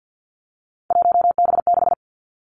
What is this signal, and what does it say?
Morse code, heard as a keyed tone, about 50 wpm, 700 Hz. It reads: 1B6
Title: Three Challenging Letter-Number Combinations 50wpm